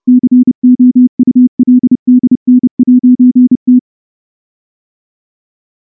Morse code, heard as a keyed tone, about 29 wpm, 250 Hz. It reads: COULDN'T
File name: morse / morse-training-30WPM